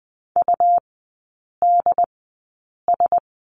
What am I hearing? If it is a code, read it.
Morse code, heard as a keyed tone, about 20 wpm, spaced as for 8 wpm, 700 Hz. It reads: UDS
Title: 107 Lesson 18e - Letter m - Three Characters 20wpm